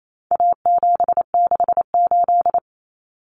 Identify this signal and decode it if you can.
Morse code, heard as a keyed tone, about 28 wpm, 700 Hz. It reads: A768
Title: Common Cut Numbers - 4-Digit 28wpm